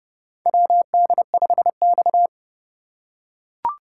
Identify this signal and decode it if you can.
Morse code, heard as a keyed tone, about 30 wpm, 700 Hz. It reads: WD5X
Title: USA Call Signs - 4 Characters - 3x Character Spacing - 30wpm